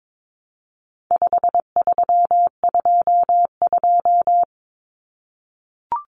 Morse code, heard as a keyed tone, about 22 wpm, 700 Hz. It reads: 5322
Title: Common Cut Numbers - 4-Digit 22wpm